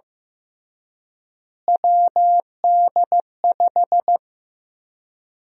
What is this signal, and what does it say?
Morse code, heard as a keyed tone, about 15 wpm, 700 Hz. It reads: WD5
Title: USA Call Sign Prefixes - Repeated 2x - 15wpm